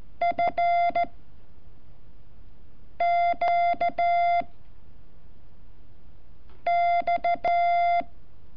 Morse code, heard as a keyed tone, about 14 wpm, 690 Hz. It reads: F Q X
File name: curso / ejercicio 29